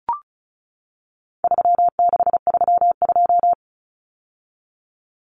Morse code, heard as a keyed tone, about 34 wpm, 700 Hz. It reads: 3632
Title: Common Cut Numbers - 4-Digit 35wpm